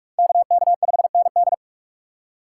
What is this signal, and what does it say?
Morse code, heard as a keyed tone, about 47 wpm, 690 Hz. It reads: KK5ND
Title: USA Call Signs - 5 Characters - 2x Character Spacing - 45wpm